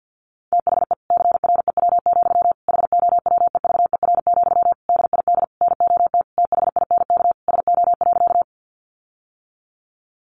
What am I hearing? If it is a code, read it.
Morse code, heard as a keyed tone, about 50 wpm, 700 Hz. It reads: THE CREW, HOWEVER, DID NOT THINK SO.
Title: Sentences from Top 1500 Words 50wpm